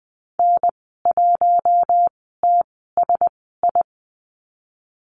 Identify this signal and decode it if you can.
Morse code, heard as a keyed tone, about 20 wpm, spaced as for 14 wpm, 700 Hz. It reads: N1TSI
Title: 042 Lesson 8g - Number 1 - Call Signs 20wpm